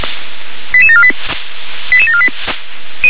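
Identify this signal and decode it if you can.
Five-tone selective call (ZVEI), heard as quick runs of five tones, tones of about 70 ms. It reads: 88748 88748
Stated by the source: ZVEI five-tone selective call received on SDR.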